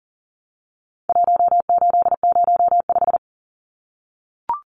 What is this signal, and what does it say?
Morse code, heard as a keyed tone, about 40 wpm, 700 Hz. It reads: 1805
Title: Numbers - Contest Style - 4-Digit 40wpm